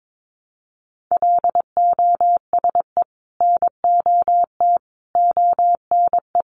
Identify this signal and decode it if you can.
Morse code, heard as a keyed tone, about 22 wpm, 700 Hz. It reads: LOSE NOT ONE
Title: Sentences from Top 1100 Words 22wpm